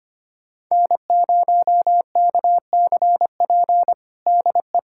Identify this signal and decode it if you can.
Morse code, heard as a keyed tone, about 25 wpm, 700 Hz. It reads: N0KCP DE
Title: USA Callsign DE Callsign 25wpm